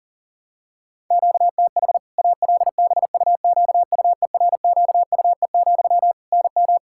Morse code, heard as a keyed tone, about 37 wpm, 700 Hz. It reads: QTH ALBUQUERQUE, NM
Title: Practice QSO - Exchange 1 - Beginning - 40wpm